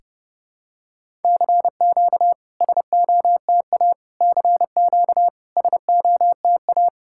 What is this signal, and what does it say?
Morse code, heard as a keyed tone, about 30 wpm, 700 Hz. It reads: CQ SOTA CQ SOTA
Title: SOTA QSO Practice 30wpm